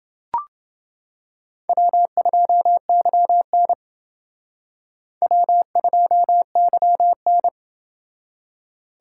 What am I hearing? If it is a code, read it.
Morse code, heard as a keyed tone, about 29 wpm, 700 Hz. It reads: W2YN W2YN